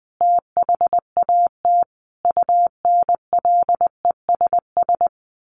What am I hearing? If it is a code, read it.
Morse code, heard as a keyed tone, about 20 wpm, 700 Hz. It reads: THAT UNLESS
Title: A Sound Of Thunder 20wpm